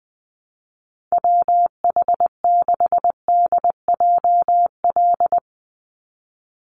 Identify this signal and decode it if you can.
Morse code, heard as a keyed tone, about 20 wpm, 700 Hz. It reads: WH6DJL
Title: USA Call Signs 20wpm